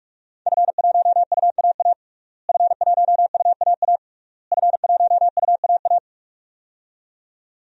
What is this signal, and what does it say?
Morse code, heard as a keyed tone, about 46 wpm, 700 Hz. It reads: F1UAA F1UAA F1UAA
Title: DXCC-100 Call Signs - 5 Characters - Repeated 3x - 45wpm